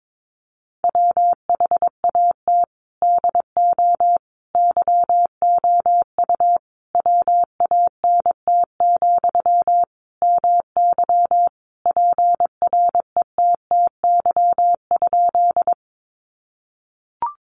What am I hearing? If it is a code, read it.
Morse code, heard as a keyed tone, about 22 wpm, 700 Hz. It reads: WHAT DO YOU WANT, MY PRETTY?
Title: Sentences from Top 1000 Words 22wpm